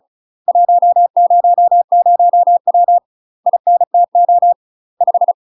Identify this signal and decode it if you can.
Morse code, heard as a keyed tone, about 36 wpm, 700 Hz. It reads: 100W INTO 5
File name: UK MASTER QSOs TRAINING / 35WPM / UK MASTER QSOs 0010